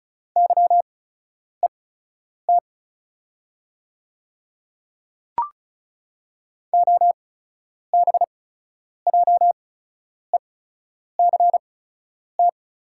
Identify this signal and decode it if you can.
Morse code, heard as a keyed tone, about 36 wpm, spaced as for 10 wpm, 690 Hz. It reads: YET OBJECT
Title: 8x Character Spacing - Top 500 Words - 35wpm